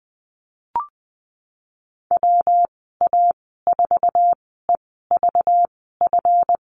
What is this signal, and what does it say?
Morse code, heard as a keyed tone, about 20 wpm, spaced as for 15 wpm, 700 Hz. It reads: WA4EVF
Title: 231 Lesson 37g - Letter x - Call Signs 20wpm